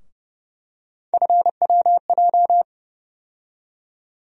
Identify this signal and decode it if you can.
Morse code, heard as a keyed tone, about 29 wpm, 700 Hz. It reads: FWJ
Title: USA Call Sign Postfixes - 3 Letters 30wpm